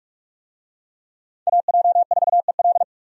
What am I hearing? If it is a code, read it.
Morse code, heard as a keyed tone, about 46 wpm, 700 Hz. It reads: AJ4EL
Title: USA Call Signs - 5 Characters - 45wpm